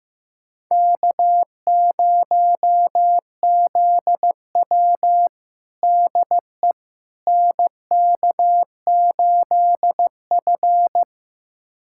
Morse code, heard as a keyed tone, about 15 wpm, 700 Hz. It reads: K0ZW DE NK8F
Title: USA Callsign DE Callsign - 4 Characters - 15wpm